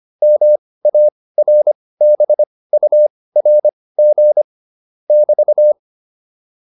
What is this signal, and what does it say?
Morse code, heard as a keyed tone, about 25 wpm, spaced as for 18 wpm, 600 Hz. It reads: MARBURG <BT>
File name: QSO405 18